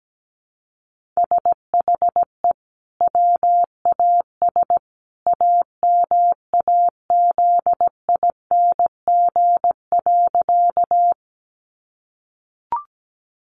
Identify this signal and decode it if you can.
Morse code, heard as a keyed tone, about 17 wpm, 700 Hz. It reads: SHE WAS AMAZING.
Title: Sentences from Top 1500 Words 17wpm